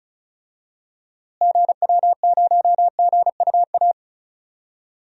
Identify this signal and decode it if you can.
Morse code, heard as a keyed tone, about 36 wpm, 700 Hz. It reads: GW0GUA
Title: DXCC-100 Call Signs - 6 Characters - Repeated 2x - 35wpm